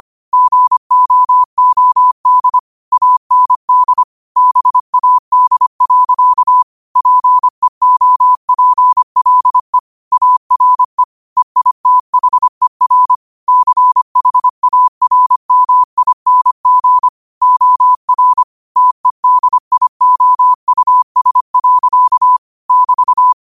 Morse code, heard as a keyed tone, about 25 wpm, 1000 Hz. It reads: GOOD AND BAD. PEOPLE ARE EITHER CHARMING OR TEDIOUS. <BT>
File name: qotdfiles / 25wpm